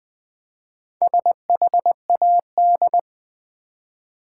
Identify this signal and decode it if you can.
Morse code, heard as a keyed tone, about 20 wpm, 700 Hz. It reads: SHAD
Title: Build A Word - 1 Letter At A Time - 3 to 5-Letter Words - 20wpm